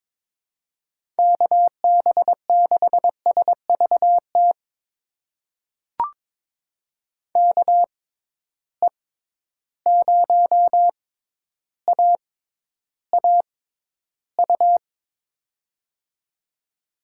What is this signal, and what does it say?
Morse code, heard as a keyed tone, about 22 wpm, spaced as for 11 wpm, 700 Hz. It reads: KB6SVT KE0AAU
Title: USA Call Signs - 6 Characters - 6x Character Spacing - 22wpm